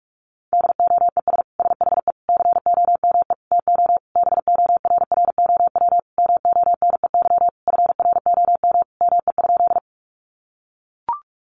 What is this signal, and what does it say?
Morse code, heard as a keyed tone, about 44 wpm, 700 Hz. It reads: DOES SHE COME TO BORROW MONEY FROM ME?
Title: Sentences from Top 1500 Words 45wpm